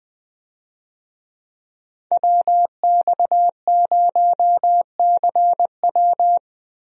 Morse code, heard as a keyed tone, about 20 wpm, 700 Hz. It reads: WX0CW
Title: USA Call Signs - 5 Characters - Repeated 3x - 20wpm